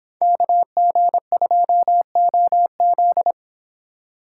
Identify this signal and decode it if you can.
Morse code, heard as a keyed tone, about 30 wpm, 700 Hz. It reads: KG2OZ